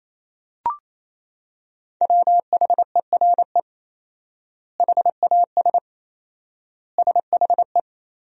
Morse code, heard as a keyed tone, about 28 wpm, 700 Hz. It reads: WHERE HAS SHE
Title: 4x Word-Spacing - Sentences from Top 300 Words - 28wpm